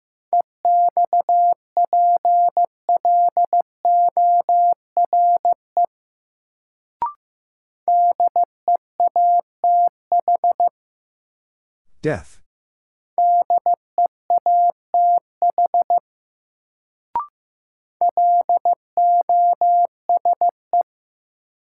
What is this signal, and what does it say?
Morse code, heard as a keyed tone, about 15 wpm, 700 Hz. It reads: EXPLORE DEATH DEATH LOSE
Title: Sentences from Top 1100 Words - New Word Drill - 15wpm